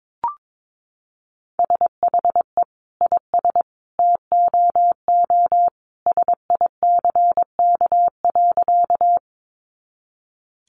Morse code, heard as a keyed tone, about 22 wpm, 700 Hz. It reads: SHE IS TOO SICK.
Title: Easy Phrases 22wpm